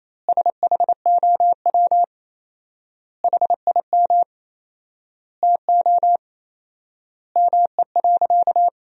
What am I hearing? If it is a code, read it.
Morse code, heard as a keyed tone, about 28 wpm, 700 Hz. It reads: SHOW HIM TO ME.
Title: 4x Word-Spacing - Sentences from Top 200 Words - 28wpm